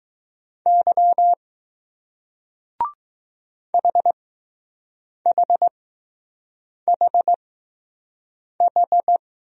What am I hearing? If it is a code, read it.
Morse code, heard as a keyed tone, about 20 wpm, spaced as for 6 wpm, 700 Hz. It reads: Y HHHH